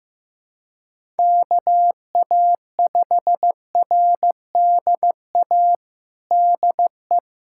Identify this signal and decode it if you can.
Morse code, heard as a keyed tone, about 15 wpm, 700 Hz. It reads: KA5RDA DE
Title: USA Callsign DE Callsign 15wpm